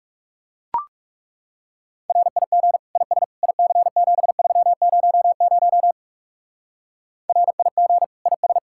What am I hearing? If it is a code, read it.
Morse code, heard as a keyed tone, about 34 wpm, 700 Hz. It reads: RIG IS IC7300 RIG IS
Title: Speed-Racing Common QSO Phrases - 30wpm